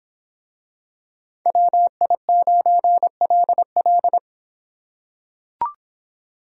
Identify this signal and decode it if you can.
Morse code, heard as a keyed tone, about 23 wpm, 700 Hz. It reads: WI9LL